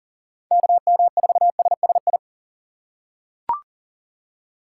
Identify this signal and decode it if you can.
Morse code, heard as a keyed tone, about 40 wpm, 700 Hz. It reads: KM4SSI
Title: USA Call Signs - 6 Characters - 6x Character Spacing - 40wpm